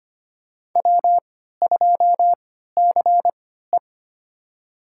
Morse code, heard as a keyed tone, about 25 wpm, spaced as for 14 wpm, 700 Hz. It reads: W2CE
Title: USA Call Signs - 4 Characters - 3x Character Spacing - 25wpm